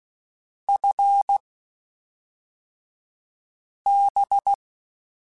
Morse code, heard as a keyed tone, about 16 wpm, spaced as for 3 wpm, 800 Hz. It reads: FB
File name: (spaced) / QSOs Spaced-mp3s / WD0FBT 1 @5 sp